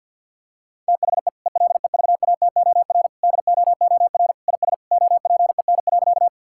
Morse code, heard as a keyed tone, about 50 wpm, 700 Hz. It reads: THE ELEVATOR DOOR IS OPEN.
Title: Sentences from Top 1600 Words 50wpm